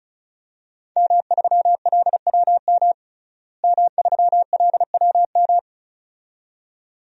Morse code, heard as a keyed tone, about 36 wpm, 700 Hz. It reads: M3LWM M3LWM
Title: DXCC-100 Call Signs - 5 Characters - Repeated 2x - 35wpm